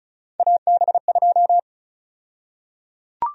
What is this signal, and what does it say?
Morse code, heard as a keyed tone, about 36 wpm, 700 Hz. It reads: AB2
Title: USA Call Sign Prefixes - 35wpm